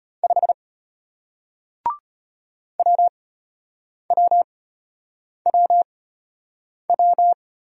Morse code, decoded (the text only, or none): 5 WWWW